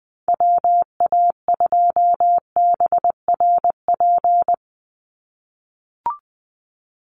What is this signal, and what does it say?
Morse code, heard as a keyed tone, about 20 wpm, 700 Hz. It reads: WA2BRP
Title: USA Call Signs 20wpm